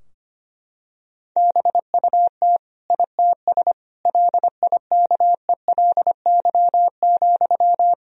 Morse code, heard as a keyed tone, about 25 wpm, 700 Hz. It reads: BUT ITS LIKELY,
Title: Sentences from Top 1000 Words - Review Part 4 - 25wpm